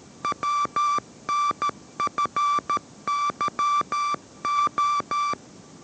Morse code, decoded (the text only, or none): WNFYO